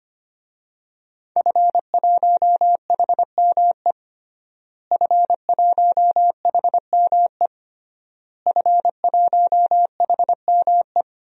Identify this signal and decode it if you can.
Morse code, heard as a keyed tone, about 25 wpm, 700 Hz. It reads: F1HME F1HME F1HME
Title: DXCC-100 Call Signs - 5 Characters - Repeated 3x - 25wpm